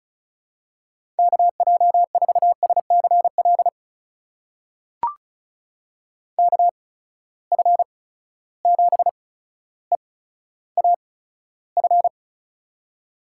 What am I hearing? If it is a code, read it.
Morse code, heard as a keyed tone, about 36 wpm, spaced as for 15 wpm, 700 Hz. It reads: KJ4SCL KF7EAF
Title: USA Call Signs - 6 Characters - 8x Character Spacing - 35wpm